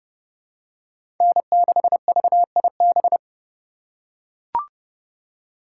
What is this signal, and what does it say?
Morse code, heard as a keyed tone, about 30 wpm, 700 Hz. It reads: N6VIB
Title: USA Call Signs - 6x Character Spacing - 30wpm